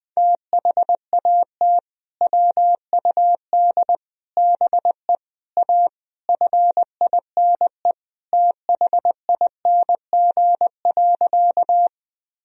Morse code, heard as a keyed tone, about 20 wpm, 700 Hz. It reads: THAT WUD BE A FINE THING.
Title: Sentences from Top 500 Words and Abbreviations 20wpm